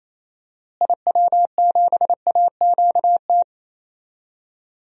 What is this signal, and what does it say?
Morse code, heard as a keyed tone, about 28 wpm, 700 Hz. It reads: IW7AQT
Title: DXCC-100 Call Signs - 6 Characters - Repeated 2x - 28wpm